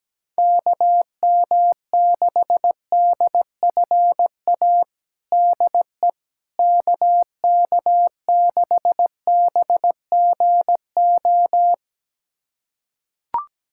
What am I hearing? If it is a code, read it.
Morse code, heard as a keyed tone, about 17 wpm, 700 Hz. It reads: KM6DFA DE KK6BGO
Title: USA Callsign DE Callsign - 6 Characters - 17wpm